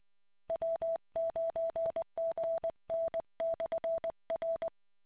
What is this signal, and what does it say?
Morse code, heard as a keyed tone, about 20 wpm, 660 Hz. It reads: W9GN/R